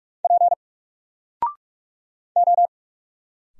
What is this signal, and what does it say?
Morse code, heard as a keyed tone, about 46 wpm, 700 Hz. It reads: P O
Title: Single Character (Letter:Number:Punctuation:Prosign) 45wpm